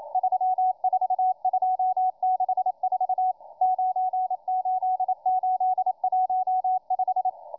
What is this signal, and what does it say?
Morse code, heard as a keyed tone, about 27 wpm, 740 Hz, with radio noise. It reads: 34264 98815